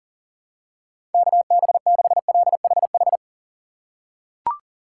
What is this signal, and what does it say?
Morse code, heard as a keyed tone, about 40 wpm, 700 Hz. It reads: KB6LHH
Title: USA Call Signs - 6 Characters - 4x Character Spacing - 40wpm